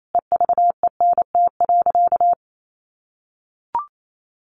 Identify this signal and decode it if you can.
Morse code, heard as a keyed tone, about 28 wpm, 700 Hz. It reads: EVENT.
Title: Sentences from Top 1400 Words 28wpm